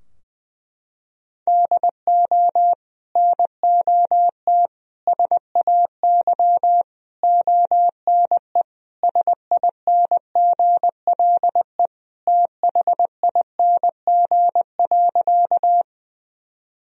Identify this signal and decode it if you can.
Morse code, heard as a keyed tone, about 20 wpm, 700 Hz. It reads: DO NOT SAY ONE SINGLE THING.
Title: Sentences from Top 1000 Words - Review Part 4 - 20wpm